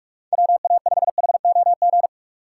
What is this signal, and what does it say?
Morse code, heard as a keyed tone, about 44 wpm, 700 Hz. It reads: WA5HOG